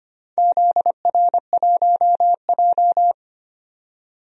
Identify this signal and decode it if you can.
Morse code, heard as a keyed tone, about 25 wpm, 700 Hz. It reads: ZR1J
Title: DXCC-100 Call Signs - 4 Characters - 25wpm